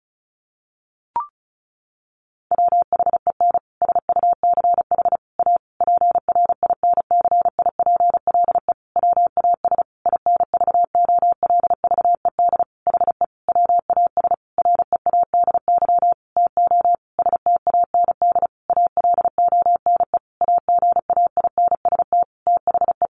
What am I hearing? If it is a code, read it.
Morse code, heard as a keyed tone, about 34 wpm, 700 Hz. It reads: WHEN SUCH A PRINCIPLE WAS INVOLVED HE WAS READY TO STAND ALONE AGAINST THE
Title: Sentences from Top 1400 Words 35wpm